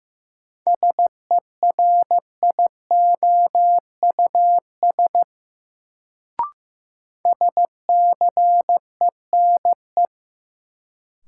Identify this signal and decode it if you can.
Morse code, heard as a keyed tone, about 16 wpm, 700 Hz. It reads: SERIOUS SCENE
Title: Sentences from Top 1100 Words - New Word Drill - 15wpm